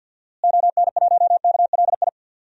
Morse code, heard as a keyed tone, about 52 wpm, 700 Hz. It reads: ON1KLI
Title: DXCC-100 Call Signs - 6 Characters - Repeated 3x - 50wpm